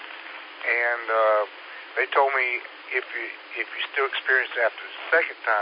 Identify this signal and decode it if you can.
Emergency Alert System (SAME) tones, heard as harsh buzzing bursts, with speech. It is heard from 0.6 to 4.5 s.